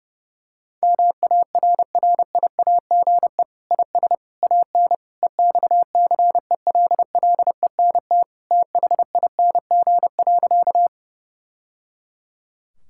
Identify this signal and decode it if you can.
Morse code, heard as a keyed tone, about 30 wpm, 700 Hz. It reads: MARRIAGE IS AN EXCELLENT THING.
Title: Sentences from Top 1000 Words - Review Part 5 - 30wpm